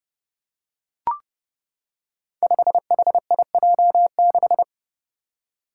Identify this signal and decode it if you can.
Morse code, heard as a keyed tone, about 30 wpm, 700 Hz. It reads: 5HIJ6
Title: Five Challenging Letter-Number Combinations 30wpm